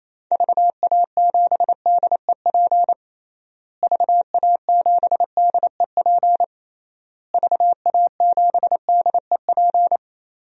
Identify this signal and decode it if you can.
Morse code, heard as a keyed tone, about 28 wpm, 700 Hz. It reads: VA7DEP VA7DEP VA7DEP
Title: DXCC-100 Call Signs - 6 Characters - Repeated 3x - 28wpm